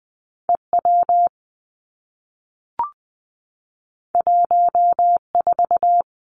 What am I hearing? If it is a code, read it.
Morse code, heard as a keyed tone, about 19 wpm, 700 Hz. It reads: E W 1 4
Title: Two Challenging Letter-Number Combinations 20wpm